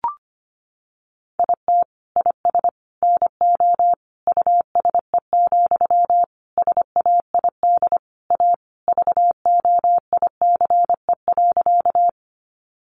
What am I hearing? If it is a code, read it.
Morse code, heard as a keyed tone, about 25 wpm, 700 Hz. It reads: IT IS NO USE, SAID A VOICE.
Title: Sentences from Top 500 Words 25wpm